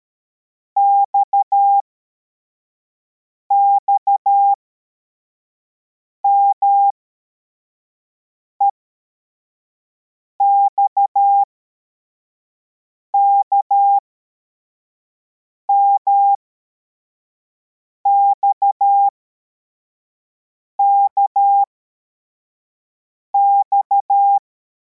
Morse code, heard as a keyed tone, about 13 wpm, spaced as for 4 wpm, 800 Hz. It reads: XXMEXKMXKX